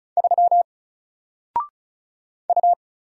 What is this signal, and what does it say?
Morse code, heard as a keyed tone, about 36 wpm, 700 Hz. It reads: 3 U
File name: Single-Letter-Number-35wpm